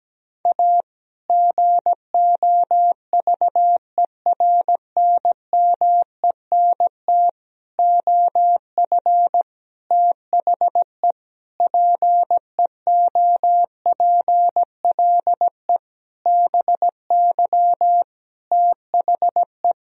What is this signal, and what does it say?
Morse code, heard as a keyed tone, about 17 wpm, 700 Hz. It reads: A GOVERNMENT OF THE PEOPLE BY THE
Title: Sentences from Top 1000 Words - Review Part 3 - 17wpm